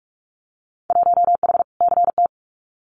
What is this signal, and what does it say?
Morse code, heard as a keyed tone, about 44 wpm, 700 Hz. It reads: 1H CT
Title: Winter Field Day Exchanges 45wpm